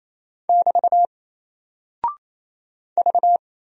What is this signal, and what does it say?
Morse code, heard as a keyed tone, about 27 wpm, 700 Hz. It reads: <BT> V
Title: Single Character (Letter:Number:Punctuation:Prosign) 28wpm